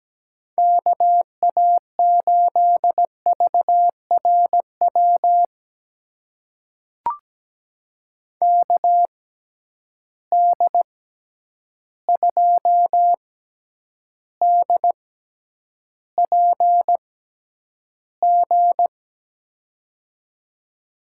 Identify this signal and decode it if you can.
Morse code, heard as a keyed tone, about 17 wpm, spaced as for 9 wpm, 700 Hz. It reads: KA8VRW KD2DPG
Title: USA Call Signs - 6 Characters - 6x Character Spacing - 17wpm